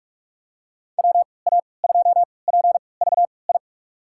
Morse code, heard as a keyed tone, about 48 wpm, spaced as for 26 wpm, 700 Hz. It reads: WA2PVI
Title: USA Call Signs - 6 Characters - 3x Character Spacing - 45wpm